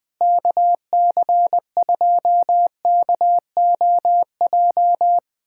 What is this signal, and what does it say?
Morse code, heard as a keyed tone, about 21 wpm, 700 Hz. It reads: KC2KOJ
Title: USA Callsign DE Callsign 20wpm